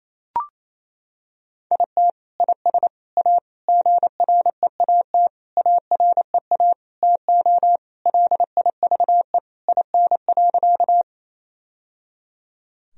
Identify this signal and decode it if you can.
Morse code, heard as a keyed tone, about 28 wpm, 700 Hz. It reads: IT IS A GREAT AREA TO LIVE IN.
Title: Sentences from Top 1000 Words - Review Part 2 - 28wpm